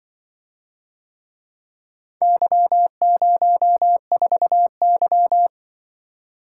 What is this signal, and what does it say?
Morse code, heard as a keyed tone, about 24 wpm, 700 Hz. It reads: Y04Y
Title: Speed-Racing - Four Letter-Number Combinations - 20wpm